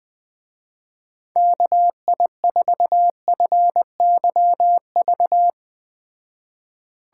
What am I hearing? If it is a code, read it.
Morse code, heard as a keyed tone, about 20 wpm, 700 Hz. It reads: KI4FYV
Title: Speed Racing - USA Call Signs 17wpm